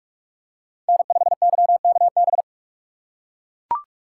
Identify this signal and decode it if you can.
Morse code, heard as a keyed tone, about 43 wpm, 700 Hz. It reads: N5YKB